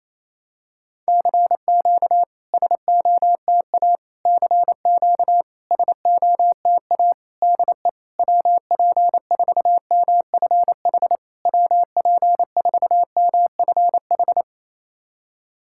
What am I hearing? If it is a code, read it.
Morse code, heard as a keyed tone, about 28 wpm, 700 Hz. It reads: CQ SOTA CQ SOTA DE WP4MFH WP4MFH
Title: SOTA QSO Practice 28wpm